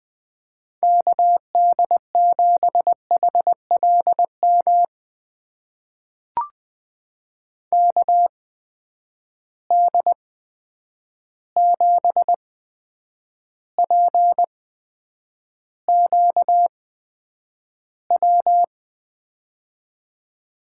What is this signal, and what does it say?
Morse code, heard as a keyed tone, about 20 wpm, spaced as for 9 wpm, 700 Hz. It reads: KD7HLM KD7PQW
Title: USA Call Signs - 6 Characters - 8x Character Spacing - 20wpm